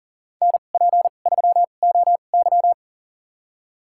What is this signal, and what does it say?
Morse code, heard as a keyed tone, about 40 wpm, spaced as for 29 wpm, 700 Hz. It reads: NP3OY
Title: USA Call Signs - 2x Character Spacing - 40wpm